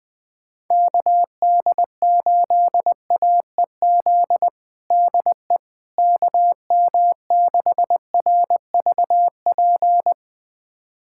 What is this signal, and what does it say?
Morse code, heard as a keyed tone, about 21 wpm, 700 Hz. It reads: KD8AEZ DE KM6RVP
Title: USA Callsign DE Callsign - 6 Characters - 20wpm